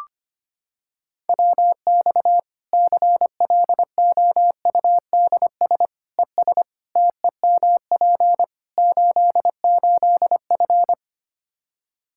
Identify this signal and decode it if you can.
Morse code, heard as a keyed tone, about 25 wpm, 700 Hz. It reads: WX CLOUDS ES TEMP 88F
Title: K1USN SST QSO Practice 25wpm